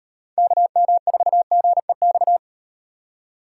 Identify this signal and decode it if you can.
Morse code, heard as a keyed tone, about 36 wpm, 700 Hz. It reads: KM4GEX